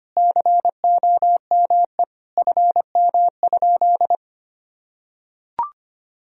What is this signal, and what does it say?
Morse code, heard as a keyed tone, about 25 wpm, 700 Hz. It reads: COME FM?
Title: Sentences from Top 300 Words and Abbreviations 25wpm